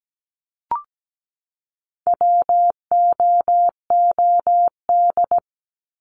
Morse code, heard as a keyed tone, about 17 wpm, 700 Hz. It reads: WOOD
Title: Top 300 Words 17wpm